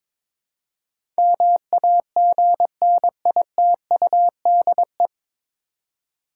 Morse code, heard as a keyed tone, about 22 wpm, 700 Hz. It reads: MAGNITUDE
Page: Morse Code Ninja practice 22wpm